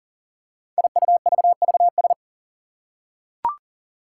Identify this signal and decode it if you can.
Morse code, heard as a keyed tone, about 40 wpm, 700 Hz. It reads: IUVVS
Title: Five Challenging Letter-Number Combinations 40wpm